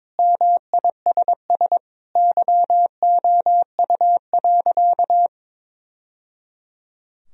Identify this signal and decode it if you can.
Morse code, heard as a keyed tone, about 22 wpm, 700 Hz. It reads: MISS YOU.
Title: Sentences from Top 1000 Words - Review Part 3 - 22wpm